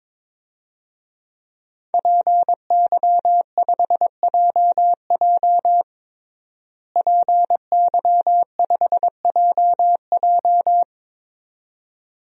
Morse code, heard as a keyed tone, about 22 wpm, 700 Hz. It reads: PY5JJ PY5JJ
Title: DXCC-100 Call Signs - 5 Characters - Repeated 2x - 22wpm